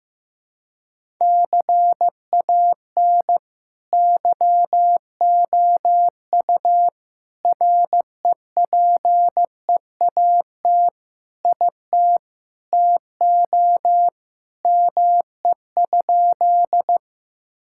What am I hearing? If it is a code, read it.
Morse code, heard as a keyed tone, about 16 wpm, 700 Hz. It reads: CAN YOU REPEAT IT TO ME?
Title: Sentences from Top 1600 Words 15wpm